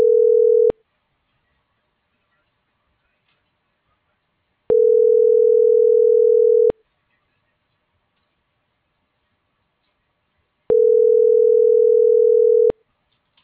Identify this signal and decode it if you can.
Ringback tone (North America), 440 + 480 Hz.